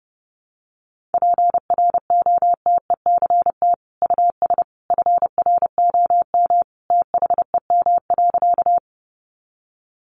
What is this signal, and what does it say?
Morse code, heard as a keyed tone, about 30 wpm, 700 Hz. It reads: PROTECT US FROM THEM.
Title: Sentences from Top 900 Words 30wpm